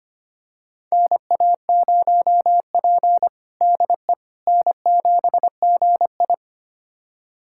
Morse code, heard as a keyed tone, about 25 wpm, 700 Hz. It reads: NA0P DE N7GI
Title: USA Callsign DE Callsign - 4 Characters - 25wpm